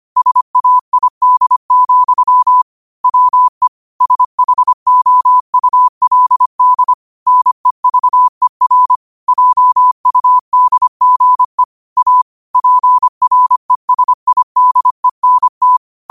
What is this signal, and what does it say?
Morse code, heard as a keyed tone, about 25 wpm, 1000 Hz. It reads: SAID, WE SHOULD NEVER JUDGE A PRESIDENT